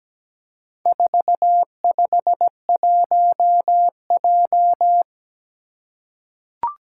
Morse code, heard as a keyed tone, about 17 wpm, 700 Hz. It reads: 451J
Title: Four Challenging Letter-Number Combinations 17wpm